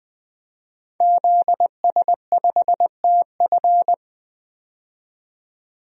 Morse code, heard as a keyed tone, about 21 wpm, 700 Hz. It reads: ZS5TF
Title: DXCC-100 Call Signs - 5 Characters - 20wpm